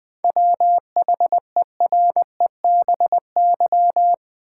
Morse code, heard as a keyed tone, about 20 wpm, 700 Hz. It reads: WHEREBY
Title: 7 Letter Words - First 3 Letters Given - 20wpm